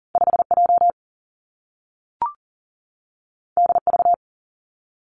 Morse code, heard as a keyed tone, about 40 wpm, 700 Hz. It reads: 5J DV
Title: Two Challenging Letter-Number Combinations 40wpm